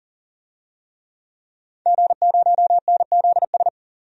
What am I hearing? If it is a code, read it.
Morse code, heard as a keyed tone, about 40 wpm, 700 Hz. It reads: G0NZS
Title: DXCC-100 Call Signs - 5 Characters - 40wpm